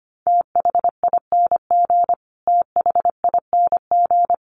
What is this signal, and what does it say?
Morse code, heard as a keyed tone, about 25 wpm, 700 Hz. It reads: THING THING
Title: Top 300 Words Repeated 25wpm